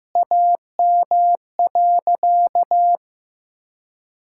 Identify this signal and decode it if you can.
Morse code, heard as a keyed tone, about 15 wpm, 700 Hz. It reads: AM.
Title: Sentences from Top 1000 Words - Review Part 5 - 15wpm